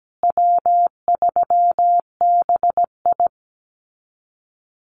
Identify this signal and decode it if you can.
Morse code, heard as a keyed tone, about 17 wpm, 700 Hz. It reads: W3BI
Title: Call Signs Repeated 2x - 17wpm